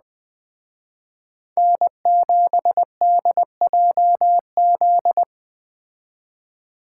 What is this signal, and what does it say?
Morse code, heard as a keyed tone, about 21 wpm, 700 Hz. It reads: N7DJZ
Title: Speed Racing - USA 1x3 Call Signs 17wpm